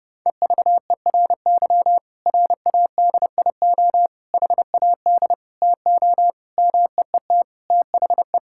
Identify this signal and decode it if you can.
Morse code, heard as a keyed tone, about 30 wpm, 700 Hz. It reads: EVERY RADIO HAD TO MEET THE
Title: Sentences from Top 1000 Words - Review Part 6 - 30wpm